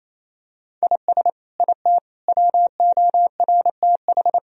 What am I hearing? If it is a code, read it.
Morse code, heard as a keyed tone, about 28 wpm, 700 Hz. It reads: IS IT WORTH
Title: Sentences from Top 1000 Words - Review Part 1 - 28wpm